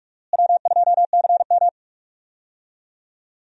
Morse code, heard as a keyed tone, about 45 wpm, 700 Hz. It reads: W2CM
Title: USA Call Signs - 4 Characters - Repeated 2x - 45wpm